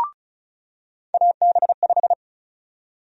Morse code, heard as a keyed tone, about 36 wpm, 700 Hz. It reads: AB5
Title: USA Call Sign Prefixes - 35wpm